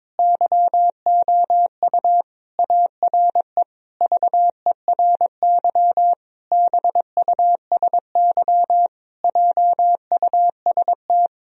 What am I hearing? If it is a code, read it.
Morse code, heard as a keyed tone, about 22 wpm, 700 Hz. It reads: YOU ARE VERY BUSY JUST
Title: Sentences from Top 1000 Words - Review Part 3 - 22wpm